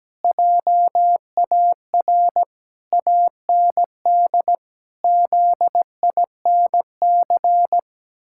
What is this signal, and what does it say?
Morse code, heard as a keyed tone, about 17 wpm, 700 Hz. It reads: JAR AND ZINC
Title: Pangram Challenge v2 - 17wpm